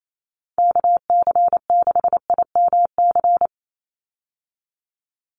Morse code, heard as a keyed tone, about 28 wpm, 700 Hz. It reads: KC6IMC
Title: USA Call Signs 28wpm